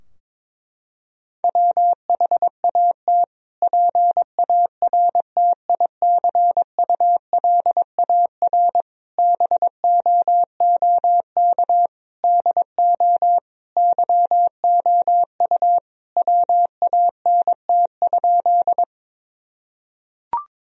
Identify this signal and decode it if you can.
Morse code, heard as a keyed tone, about 22 wpm, 700 Hz. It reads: WHAT PARTICULAR BOOK DO YOU WANT?
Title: Sentences from Top 1000 Words - Review Part 4 - 22wpm